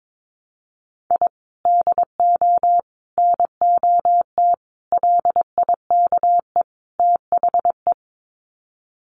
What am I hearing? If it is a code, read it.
Morse code, heard as a keyed tone, about 22 wpm, 700 Hz. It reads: I DO NOT LIKE THE
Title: Sets of 5 Words Encore 22wpm